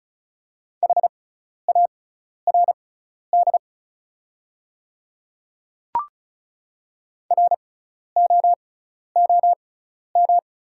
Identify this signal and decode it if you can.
Morse code, heard as a keyed tone, about 36 wpm, spaced as for 12 wpm, 690 Hz. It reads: HARD ROOM
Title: 6x Character Spacing - Top 300 Words - 35wpm